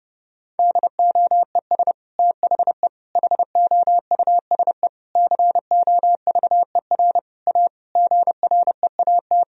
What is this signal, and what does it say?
Morse code, heard as a keyed tone, about 30 wpm, 700 Hz. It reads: DOES THE HOUSE COVER A GREAT
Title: Sentences from Top 1000 Words - Review Part 2 - 30wpm